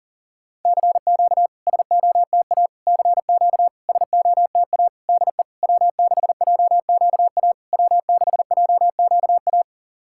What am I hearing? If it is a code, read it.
Morse code, heard as a keyed tone, about 43 wpm, 700 Hz. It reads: CQ SOTA CQ SOTA DE W6JQA W6JQA
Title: SOTA QSO Practice 40wpm